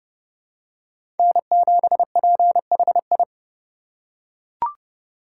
Morse code, heard as a keyed tone, about 32 wpm, 700 Hz. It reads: N7PHI
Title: K1USN SST QSO Practice - 30wpm